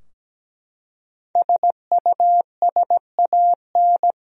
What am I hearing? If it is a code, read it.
Morse code, heard as a keyed tone, about 17 wpm, 700 Hz. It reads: SUSAN
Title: USA Common Names - Max 5 Letters - 17wpm